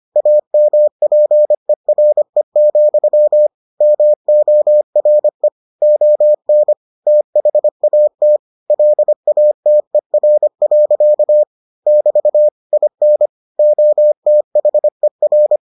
Morse code, heard as a keyed tone, about 25 wpm, 600 Hz. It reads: AMPERE, MORE ON THAT LATER. <BT> IN OTHER